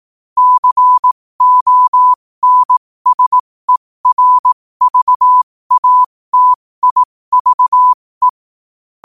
Morse code, heard as a keyed tone, about 18 wpm, spaced as for 15 wpm, 1000 Hz. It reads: CONSERVATIVE
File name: qotdfiles / 15wpm